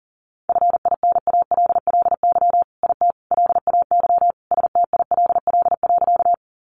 Morse code, heard as a keyed tone, about 40 wpm, 700 Hz. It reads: FINALLY IT LAY STILL.
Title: Sentences from Top 1000 Words 40wpm